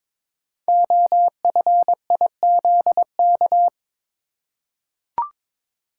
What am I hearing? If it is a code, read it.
Morse code, heard as a keyed tone, about 22 wpm, 700 Hz. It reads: OFIZK
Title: Code Groups - 5 Letters - 22wpm